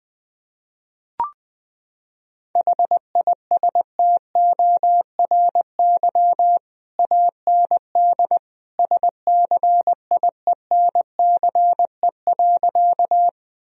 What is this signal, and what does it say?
Morse code, heard as a keyed tone, about 20 wpm, 700 Hz. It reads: HISTORY AND SCIENCE.
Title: Sentences from Top 1000 Words - Review Part 2 - 20wpm